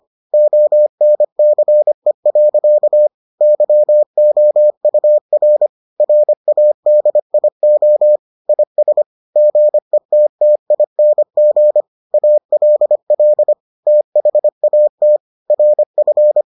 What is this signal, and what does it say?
Morse code, heard as a keyed tone, about 25 wpm, 600 Hz. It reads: ONCE. YOUR RADIO IS GETTING ALL THAT RF